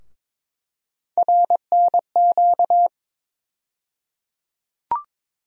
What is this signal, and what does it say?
Morse code, heard as a keyed tone, about 22 wpm, 700 Hz. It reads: RNQ
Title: USA Call Sign Postfixes - 3 Letters 22wpm